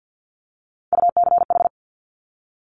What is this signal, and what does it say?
Morse code, heard as a keyed tone, about 50 wpm, 700 Hz. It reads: UCH
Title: Top Three Letter Combinations in Words 50wpm